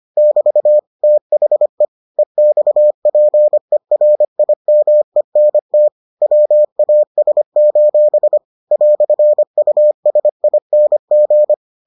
Morse code, heard as a keyed tone, about 25 wpm, 600 Hz. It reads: <BT> THE EXPERIMENT WAS: "USING